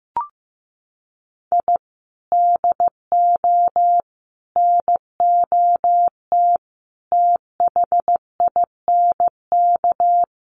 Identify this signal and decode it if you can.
Morse code, heard as a keyed tone, about 15 wpm, 700 Hz. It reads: I DO NOT THINK
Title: Sets of 5 Words 15wpm